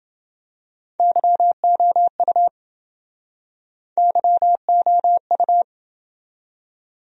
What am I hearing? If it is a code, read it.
Morse code, heard as a keyed tone, about 27 wpm, 700 Hz. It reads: YOU YOU